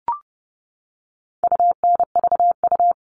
Morse code, heard as a keyed tone, about 30 wpm, 700 Hz. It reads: UNVU
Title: All Cut Numbers - 4-Digit 30wpm